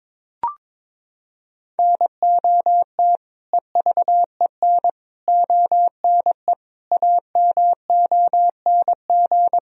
Morse code, heard as a keyed tone, about 22 wpm, 700 Hz. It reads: NOT EVEN ONE AMONG
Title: Sentences from Top 1000 Words - Review Part 3 - 22wpm